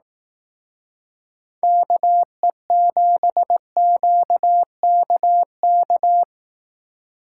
Morse code, heard as a keyed tone, about 18 wpm, 700 Hz. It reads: KE7QKK
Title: Speed Racing - USA 2x3 Call Signs 15wpm